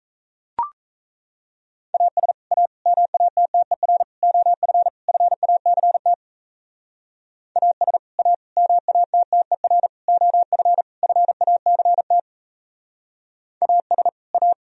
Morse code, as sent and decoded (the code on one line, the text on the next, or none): .- ... / .- / -- .- - - . .-. / --- ..-. / ..-. .- -.-. - / .- ... / .- / -- .- - - . .-. / --- ..-. / ..-. .- -.-. - / .- ... / .-
AS A MATTER OF FACT AS A MATTER OF FACT AS A